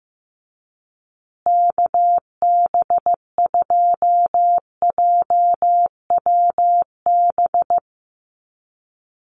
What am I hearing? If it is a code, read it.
Morse code, heard as a keyed tone, about 15 wpm, 700 Hz. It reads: KB2JWB
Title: USA Call Signs Repeated 3x 15wpm